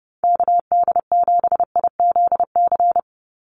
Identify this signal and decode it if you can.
Morse code, heard as a keyed tone, about 30 wpm, 700 Hz. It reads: KD7IZC
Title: USA Call Signs 30wpm